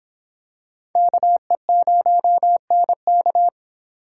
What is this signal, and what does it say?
Morse code, heard as a keyed tone, about 26 wpm, 700 Hz. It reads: KE0NK